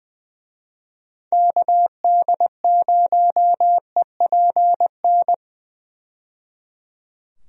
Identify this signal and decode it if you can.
Morse code, heard as a keyed tone, about 20 wpm, 700 Hz. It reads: KD0EPN
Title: USA Call Signs - 6 Characters - 20wpm